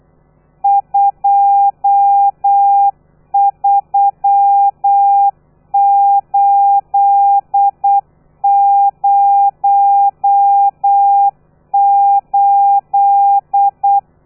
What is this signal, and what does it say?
Morse code, heard as a keyed tone, about 8 wpm, 800 Hz. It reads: 23808
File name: telegrafia / i2tfj / Lez12 35